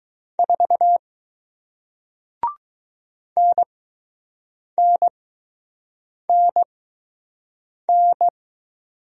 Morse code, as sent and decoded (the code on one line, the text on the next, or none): ....- / -. -. -. -.
4 NNNN